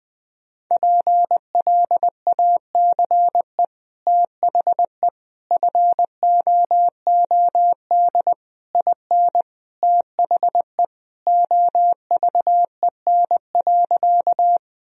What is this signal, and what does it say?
Morse code, heard as a keyed tone, about 20 wpm, 700 Hz. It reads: PLACE THE FOOD IN THE OVEN.
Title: Sentences from Top 1000 Words - Review Part 4 - 20wpm